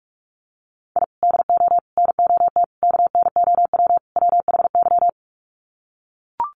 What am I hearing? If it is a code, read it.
Morse code, heard as a keyed tone, about 44 wpm, 700 Hz. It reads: I DO NOT KNOW WHY
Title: Sets of 5 Words 45wpm